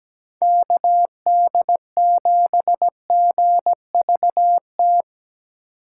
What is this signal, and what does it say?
Morse code, heard as a keyed tone, about 17 wpm, 700 Hz. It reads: KD7GVT
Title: USA Call Signs - 6 Characters - 2x Character Spacing - 17wpm